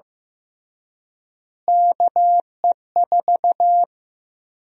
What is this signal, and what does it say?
Morse code, heard as a keyed tone, about 15 wpm, 700 Hz. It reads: KE4
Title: USA Call Sign Prefixes - Repeated 2x - 15wpm